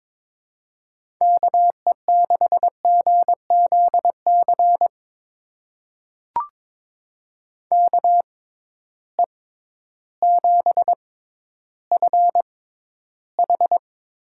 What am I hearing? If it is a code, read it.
Morse code, heard as a keyed tone, about 22 wpm, spaced as for 12 wpm, 700 Hz. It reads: KE6GZC KE7FH
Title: USA Call Signs - 6 Characters - 6x Character Spacing - 22wpm